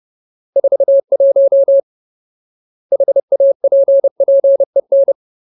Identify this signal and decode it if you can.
Morse code, heard as a keyed tone, about 30 wpm, 550 Hz. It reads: 41 HAPPEN
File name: Ultimate Word List / 30WPM Ultimate